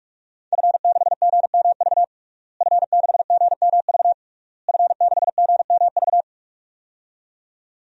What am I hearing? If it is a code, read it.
Morse code, heard as a keyed tone, about 41 wpm, 700 Hz. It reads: F6GMV F6GMV F6GMV
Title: DXCC-100 Call Signs - 5 Characters - Repeated 3x - 45wpm